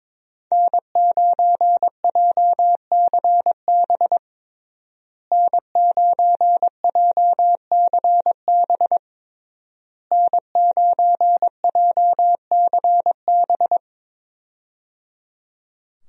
Morse code, heard as a keyed tone, about 21 wpm, 700 Hz. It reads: N9JCB N9JCB N9JCB
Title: USA Call Signs - 5 Characters - Repeated 3x - 22wpm